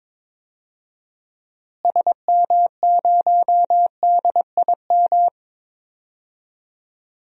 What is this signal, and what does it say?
Morse code, heard as a keyed tone, about 22 wpm, 700 Hz. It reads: SM0DIM
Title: DXCC-100 Call Signs - 6 Characters - Repeated 2x - 22wpm